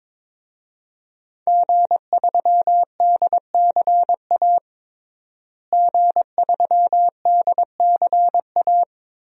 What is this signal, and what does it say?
Morse code, heard as a keyed tone, about 22 wpm, 700 Hz. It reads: G3DCA G3DCA
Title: DXCC-100 Call Signs - 5 Characters - Repeated 3x - 22wpm